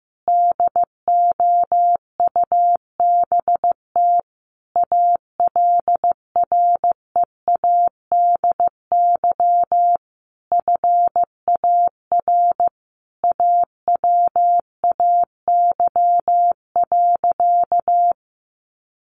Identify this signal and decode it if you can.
Morse code, heard as a keyed tone, about 15 wpm, 700 Hz. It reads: DOUBT ALREADY FAR AWAY.
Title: Sentences from Top 1500 Words 15wpm